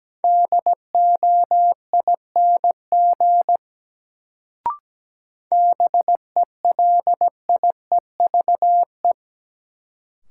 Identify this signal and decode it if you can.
Morse code, heard as a keyed tone, about 17 wpm, 700 Hz. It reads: DOING BELIEVE
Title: Sets of 4 Words - New Word Drill - 17wpm